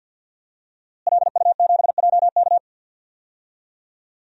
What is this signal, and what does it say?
Morse code, heard as a keyed tone, about 52 wpm, 700 Hz. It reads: LU7JK
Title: DXCC-100 Call Signs - 5 Characters - Repeated 2x - 50wpm